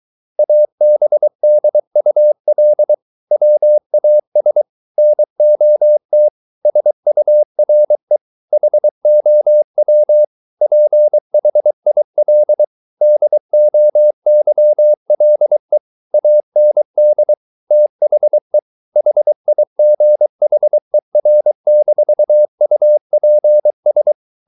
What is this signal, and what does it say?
Morse code, heard as a keyed tone, about 23 wpm, 600 Hz. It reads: ABDUL WAS NOT SURE HOW PHIL DOYLE AND THE HIGHER-UPS